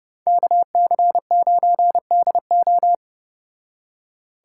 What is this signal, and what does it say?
Morse code, heard as a keyed tone, about 34 wpm, 700 Hz. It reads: KC9DO